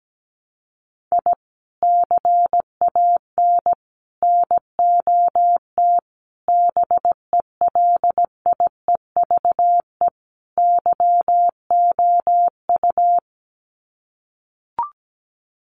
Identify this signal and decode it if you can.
Morse code, heard as a keyed tone, about 17 wpm, 700 Hz. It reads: I CAN NOT BELIEVE YOU
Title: Sets of 5 Words Encore 17wpm